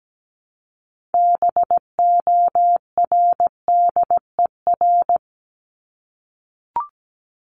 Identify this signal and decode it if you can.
Morse code, heard as a keyed tone, about 17 wpm, 700 Hz. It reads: BORDER
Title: Top 1751-2000 Words 17wpm